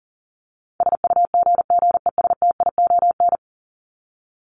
Morse code, heard as a keyed tone, about 40 wpm, 700 Hz. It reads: SUGGESTION
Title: Top 1501-1750 Words 40wpm